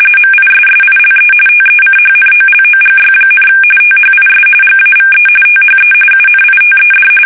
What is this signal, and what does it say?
RTTY, about 45.45 baud. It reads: MISSILE TECHNOLOGY, THIS GAVE THE UNITED
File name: fsk850 / usnavy5